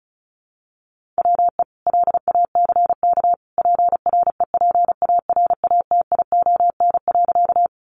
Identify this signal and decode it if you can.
Morse code, heard as a keyed tone, about 34 wpm, 700 Hz. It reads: WE LACK PREPARATION.
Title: Sentences from Top 1400 Words 35wpm